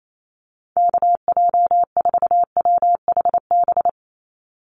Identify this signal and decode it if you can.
Morse code, heard as a keyed tone, about 28 wpm, 700 Hz. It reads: KJ4WHB
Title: USA Call Signs 28wpm